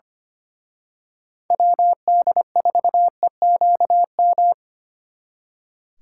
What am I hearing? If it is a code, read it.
Morse code, heard as a keyed tone, about 29 wpm, 700 Hz. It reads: WD4EQM
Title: Speed Racing - USA 2x3 Call Signs 25wpm